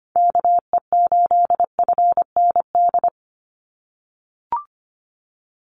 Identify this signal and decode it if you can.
Morse code, heard as a keyed tone, about 25 wpm, 700 Hz. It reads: KE8FND
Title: USA Call Signs 25wpm